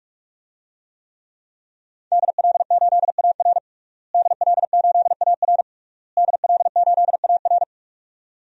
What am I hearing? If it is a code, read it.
Morse code, heard as a keyed tone, about 41 wpm, 700 Hz. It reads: DL8AR DL8AR DL8AR
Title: DXCC-100 Call Signs - 5 Characters - Repeated 3x - 45wpm